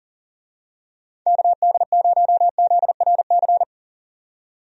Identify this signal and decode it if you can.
Morse code, heard as a keyed tone, about 40 wpm, 700 Hz. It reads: KD0ZRC
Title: USA Call Signs - 6 Characters - 40wpm